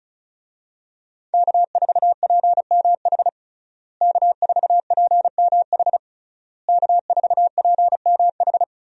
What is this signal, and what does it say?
Morse code, heard as a keyed tone, about 34 wpm, 700 Hz. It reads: K4PMH K4PMH K4PMH
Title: USA Call Signs - 5 Characters - Repeated 3x - 35wpm